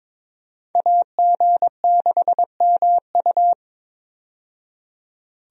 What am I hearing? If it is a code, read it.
Morse code, heard as a keyed tone, about 22 wpm, 700 Hz. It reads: AG6MU
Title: USA Call Signs - 5 Characters - 22wpm